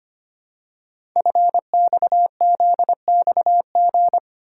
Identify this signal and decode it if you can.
Morse code, heard as a keyed tone, about 25 wpm, 700 Hz. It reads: FXZXG
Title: Code Groups - 5 Letters - 25wpm